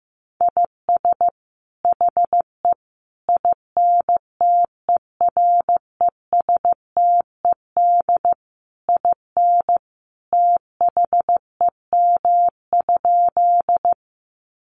Morse code, heard as a keyed tone, about 15 wpm, 700 Hz. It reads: IS HE INTERESTED IN THEM?
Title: Sentences from Top 900 Words 15wpm